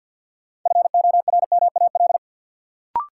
Morse code, heard as a keyed tone, about 52 wpm, 700 Hz. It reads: FORMAL
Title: Sentences from Top 1300 Words - New Word Drill - 50wpm